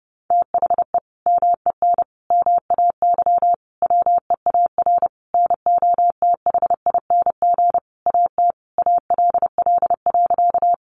Morse code, heard as a keyed tone, about 30 wpm, 700 Hz. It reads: THE MEN MAY WEAR NOTHING AT ALL.
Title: Sentences from Top 1500 Words 30wpm